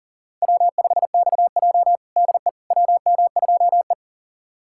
Morse code, heard as a keyed tone, about 40 wpm, 700 Hz. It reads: W5XJ DE WM2E
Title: USA Callsign DE Callsign - 4 Characters - 40wpm